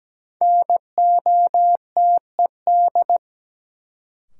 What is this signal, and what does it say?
Morse code, heard as a keyed tone, about 17 wpm, 700 Hz. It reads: NOTED
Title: Sets of 5 Words - New Word Drill - 17wpm